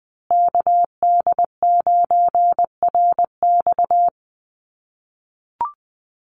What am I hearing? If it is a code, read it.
Morse code, heard as a keyed tone, about 20 wpm, 700 Hz. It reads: KD9RX
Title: USA Call Signs Repeated 3x 20wpm